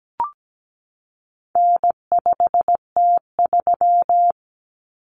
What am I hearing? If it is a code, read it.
Morse code, heard as a keyed tone, about 17 wpm, 700 Hz. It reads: N5T3
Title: Common Cut Numbers - 4-Digit 17wpm